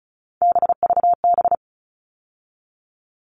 Morse code, heard as a keyed tone, about 34 wpm, 700 Hz. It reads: BVB
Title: All Cut Numbers - 3-Digit 35wpm